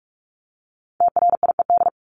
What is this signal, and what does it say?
Morse code, heard as a keyed tone, about 44 wpm, 700 Hz. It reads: TRIED
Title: ED Suffix - Top 100 Words with Base Words 45wpm